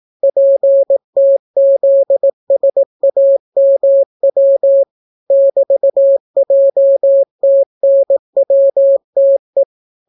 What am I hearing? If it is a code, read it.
Morse code, heard as a keyed tone, about 18 wpm, 550 Hz. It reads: PTZSAMW <BT>JTNWTE